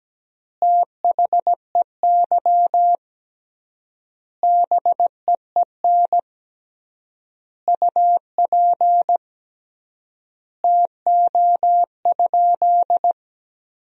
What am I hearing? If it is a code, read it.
Morse code, heard as a keyed tone, about 17 wpm, 700 Hz. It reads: THEY BEEN UP TO?
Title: 3x Word-Spacing - Sentences from Top 100 Words - 17wpm